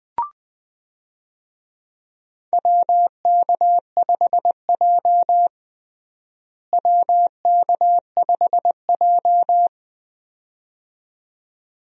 Morse code, heard as a keyed tone, about 21 wpm, 700 Hz. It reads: WK5J WK5J
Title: USA Call Signs - 4 Characters - Repeated 2x - 20wpm